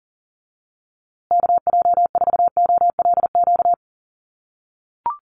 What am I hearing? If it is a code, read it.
Morse code, heard as a keyed tone, about 40 wpm, 700 Hz. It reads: KJ4OLQ
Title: USA Call Signs Repeated 2x 40wpm